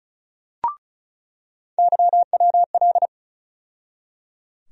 Morse code, heard as a keyed tone, about 35 wpm, 700 Hz. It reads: YWL
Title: USA Call Sign Postfixes - 3 Letters 35wpm